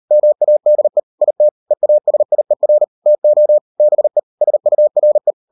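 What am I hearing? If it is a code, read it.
Morse code, heard as a keyed tone, about 38 wpm, 600 Hz. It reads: MADE IT EASIER TO BE SURE
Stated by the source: Morse code practice at 38 WPM, sending Ken Follett's novel Never in English, from CW med Gnister.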